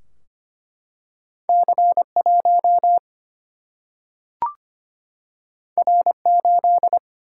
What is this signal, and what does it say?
Morse code, heard as a keyed tone, about 25 wpm, 700 Hz. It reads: C 1 R 8
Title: Single-Letter Single-Number 25wpm